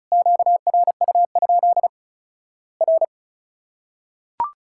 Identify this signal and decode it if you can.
Morse code, heard as a keyed tone, about 36 wpm, 700 Hz. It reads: QRU? R
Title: Worked All Europe DX Contest - QTC Exchange - 35wpm